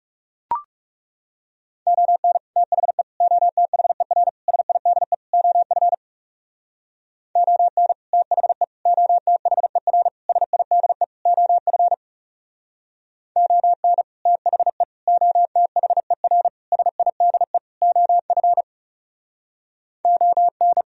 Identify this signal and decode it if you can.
Morse code, heard as a keyed tone, about 40 wpm, 700 Hz. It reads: ON THE OTHER SIDE OF ON THE OTHER SIDE OF ON THE OTHER SIDE OF ON